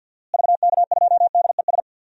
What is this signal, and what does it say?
Morse code, heard as a keyed tone, about 49 wpm, 700 Hz. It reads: VK2DES
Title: DXCC-100 Call Signs - 6 Characters - Repeated 2x - 50wpm